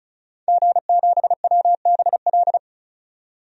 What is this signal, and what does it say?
Morse code, heard as a keyed tone, about 35 wpm, 700 Hz. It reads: G7WBL
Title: DXCC-100 Call Signs - 5 Characters - 35wpm